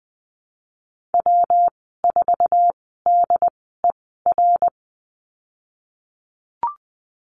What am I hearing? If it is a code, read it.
Morse code, heard as a keyed tone, about 20 wpm, spaced as for 14 wpm, 700 Hz. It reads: W4DER
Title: 173 Lesson 28g - Number 9 - Call Signs 20wpm